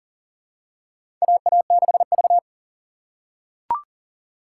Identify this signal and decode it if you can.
Morse code, heard as a keyed tone, about 40 wpm, 700 Hz. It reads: AA6V
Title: USA Call Signs - 4 Characters - 3x Character Spacing - 40wpm